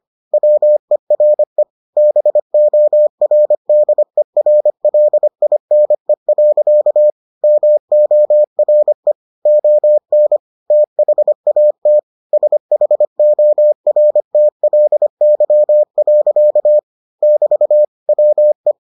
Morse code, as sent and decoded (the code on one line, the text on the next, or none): .-- . .-. / -... --- .-. -.. .-. .-.. .. -. .-.-.- / -- --- .-. / --- -. / - .... .- - / ... .... --- .-. - .-.. -.-- .-.-.- / -...- / .-- .
WERE BORDERLINE. MORE ON THAT SHORTLY. <BT> WE